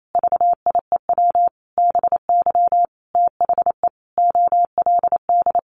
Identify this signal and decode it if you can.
Morse code, heard as a keyed tone, about 28 wpm, 700 Hz. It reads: VIEW BY THE OLD
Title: Sentences from Top 1200 Words 28wpm